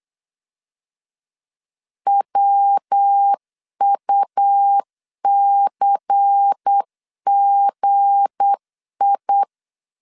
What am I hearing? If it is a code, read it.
Morse code, heard as a keyed tone, about 8 wpm, 800 Hz. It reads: WUCGI
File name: LEZIONI / 9 286 8wpm 3425